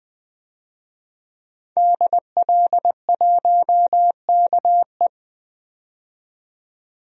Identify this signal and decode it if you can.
Morse code, heard as a keyed tone, about 20 wpm, 700 Hz. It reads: DL1KE
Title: DXCC-100 Call Signs - 5 Characters - 20wpm